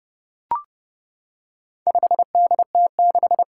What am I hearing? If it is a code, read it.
Morse code, heard as a keyed tone, about 31 wpm, 700 Hz. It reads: 5DT6
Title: Four Challenging Letter-Number Combinations 30wpm